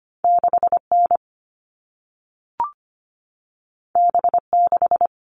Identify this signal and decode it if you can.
Morse code, heard as a keyed tone, about 25 wpm, 700 Hz. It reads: 6N B6
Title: All Cut Numbers - 2-Digit 25wpm